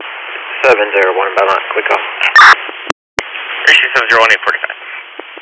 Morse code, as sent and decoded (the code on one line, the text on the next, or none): none